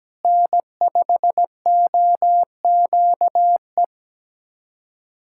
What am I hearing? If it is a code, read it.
Morse code, heard as a keyed tone, about 17 wpm, 700 Hz. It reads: N5OQE
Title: USA Callsign DE Callsign 17wpm